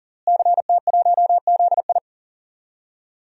Morse code, heard as a keyed tone, about 40 wpm, 700 Hz. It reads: CT1ZI
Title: DXCC-100 Call Signs - 5 Characters - Repeated 2x - 40wpm